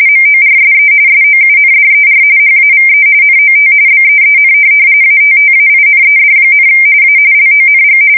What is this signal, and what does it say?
RTTY, about 45.45 baud. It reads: MORSE SOUNDER HAVE NOT BEEN ATTAINED, A LONG
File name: fsk170 / morkrum